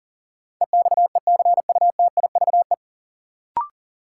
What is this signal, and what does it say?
Morse code, heard as a keyed tone, about 40 wpm, 700 Hz. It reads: EXECUTIVE
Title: Sets of 5 Words Encore - New Word Drill - 40wpm